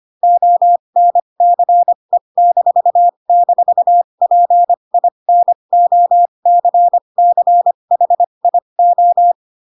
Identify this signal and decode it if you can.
Morse code, heard as a keyed tone, about 25 wpm, spaced as for 22 wpm, 700 Hz. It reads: ONCE--PINOCCHIO
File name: Pinocchio / Pinocchio ch3 18 25 ik1hge